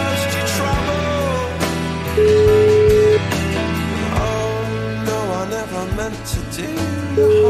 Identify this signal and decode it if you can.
Ringback tone (Europe, 425 Hz), with noise.